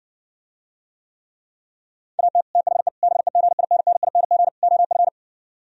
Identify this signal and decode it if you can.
Morse code, heard as a keyed tone, about 69 wpm, 700 Hz. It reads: AT THE BEGINNING OF